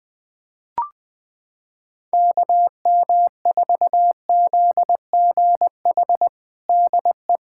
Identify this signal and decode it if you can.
Morse code, heard as a keyed tone, about 21 wpm, 700 Hz. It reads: KM4ZGH DE
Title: USA Callsign DE Callsign 20wpm